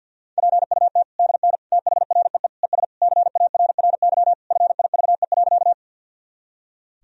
Morse code, heard as a keyed tone, about 50 wpm, 700 Hz. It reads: PUT DN THREE ES CARRY FIVE.
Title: Sentences from Top 400 Words and Abbreviations 50wpm